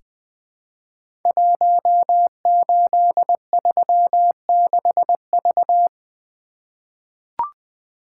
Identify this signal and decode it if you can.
Morse code, heard as a keyed tone, about 21 wpm, 700 Hz. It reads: 1836V
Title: Code Groups - 5 Letter:Numbers - 20wpm